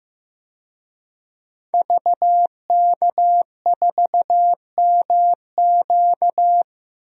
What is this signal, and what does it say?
Morse code, heard as a keyed tone, about 15 wpm, 700 Hz. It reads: VK4MQ
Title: DXCC-100 Call Signs - 5 Characters - 15wpm